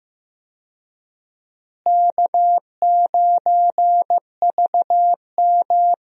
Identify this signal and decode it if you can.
Morse code, heard as a keyed tone, about 15 wpm, 700 Hz. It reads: K9VM
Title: USA Call Signs - 4 Characters - Repeated 3x - 15wpm